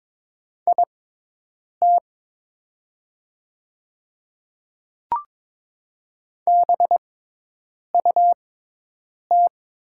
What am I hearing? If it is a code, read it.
Morse code, heard as a keyed tone, about 22 wpm, spaced as for 8 wpm, 700 Hz. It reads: IT BUT
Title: 6x Character Spacing - Top 300 Words - 22wpm